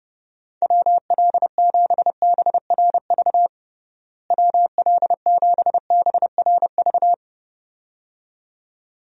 Morse code, heard as a keyed tone, about 30 wpm, 700 Hz. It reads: WL7BRV WL7BRV
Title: USA Call Signs - 6 Characters - Repeated 2x - 30wpm